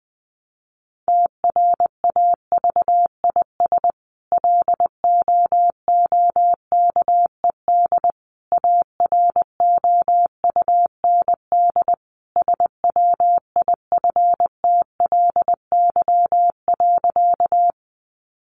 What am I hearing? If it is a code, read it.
Morse code, heard as a keyed tone, about 20 wpm, 700 Hz. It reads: TRAVIS LOOKED AROUND SWIFTLY.
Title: A Sound Of Thunder 20wpm